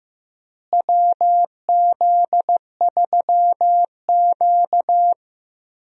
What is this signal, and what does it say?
Morse code, heard as a keyed tone, about 15 wpm, 700 Hz. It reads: WZ3Q
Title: USA Call Signs - 4 Characters - Repeated 2x - 15wpm